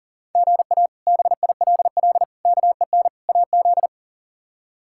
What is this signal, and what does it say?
Morse code, heard as a keyed tone, about 40 wpm, 690 Hz. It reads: GA BILL KEN AZ
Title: K1USN SST QSO Practice - 40wpm